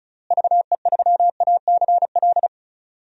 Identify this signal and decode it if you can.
Morse code, heard as a keyed tone, about 36 wpm, 700 Hz. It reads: VE3ACL
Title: DXCC-100 Call Signs - 6 Characters - Repeated 3x - 35wpm